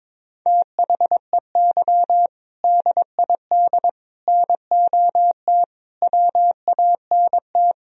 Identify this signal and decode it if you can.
Morse code, heard as a keyed tone, about 22 wpm, 700 Hz. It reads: THEY DID NOT WANT
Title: Speed-Racing - Sets of 5 Words 22wpm